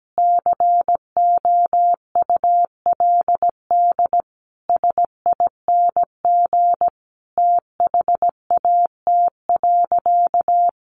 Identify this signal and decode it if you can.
Morse code, heard as a keyed tone, about 17 wpm, 700 Hz. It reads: COULD SING THAT.
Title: Sentences from Top 500 Words 17wpm